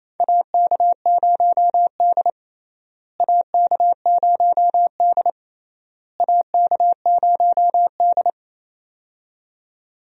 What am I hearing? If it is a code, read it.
Morse code, heard as a keyed tone, about 28 wpm, 700 Hz. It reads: AK0D AK0D AK0D
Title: USA Call Signs - 4 Characters - Repeated 3x - 28wpm